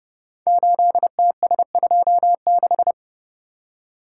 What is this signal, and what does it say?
Morse code, heard as a keyed tone, about 28 wpm, 700 Hz. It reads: 8TS26
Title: Code Groups - 5 Letter:Numbers - 30wpm